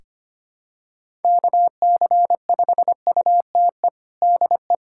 Morse code, heard as a keyed tone, about 25 wpm, 700 Hz. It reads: KC5UTE DE
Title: USA Callsign DE Callsign 25wpm